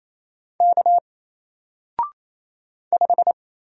Morse code, heard as a keyed tone, about 27 wpm, 700 Hz. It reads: K 5
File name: Single-Letter-Number-28wpm